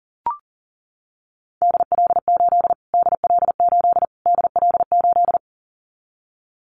Morse code, heard as a keyed tone, about 40 wpm, 700 Hz. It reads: DL8 DL8 DL8
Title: Callsign Prefixes for Top 25 DXCC - Repeated 3x - 40wpm